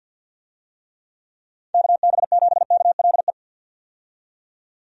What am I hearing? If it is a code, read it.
Morse code, heard as a keyed tone, about 52 wpm, 700 Hz. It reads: KB7KLE
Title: USA Call Signs - 6 Characters - 50wpm